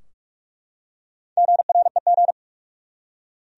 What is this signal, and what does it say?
Morse code, heard as a keyed tone, about 47 wpm, 700 Hz. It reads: GREG
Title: USA Common Names - Max 4 Letters - 45wpm